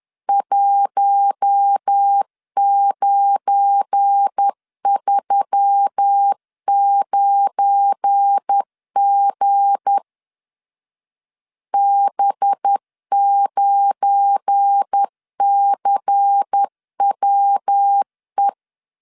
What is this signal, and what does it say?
Morse code, heard as a keyed tone, about 10 wpm, 800 Hz. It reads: 1939G B9CWE
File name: LEZIONI / 7 GF9Z / 7 GF9 10wpm 2747